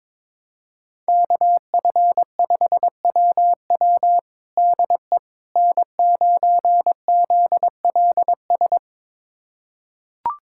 Morse code, heard as a keyed tone, about 21 wpm, 700 Hz. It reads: KF5WW DE N9ZLS
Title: USA Callsign DE Callsign - 5 Characters - 22wpm